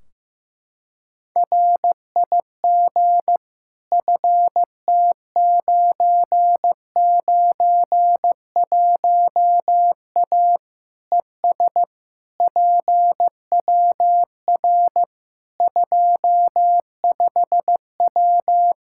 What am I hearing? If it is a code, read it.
Morse code, heard as a keyed tone, about 15 wpm, 700 Hz. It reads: RIG FT991A ES PWR 25W
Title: Practice QSO - Exchange 2 - Info - 15wpm